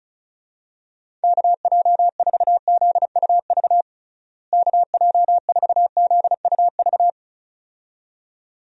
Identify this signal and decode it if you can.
Morse code, heard as a keyed tone, about 34 wpm, 700 Hz. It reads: KJ4ZUV KJ4ZUV
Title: USA Call Signs - 6 Characters - Repeated 2x - 35wpm